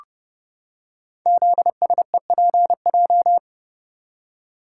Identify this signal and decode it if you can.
Morse code, heard as a keyed tone, about 30 wpm, 700 Hz. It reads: ZSEPJ
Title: Code Groups - 5 Letters - 30wpm